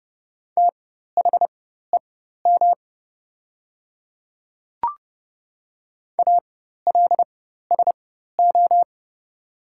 Morse code, heard as a keyed tone, about 30 wpm, spaced as for 14 wpm, 700 Hz. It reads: THEM ALSO
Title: 4x Character Spacing - Top 300 Words - 30wpm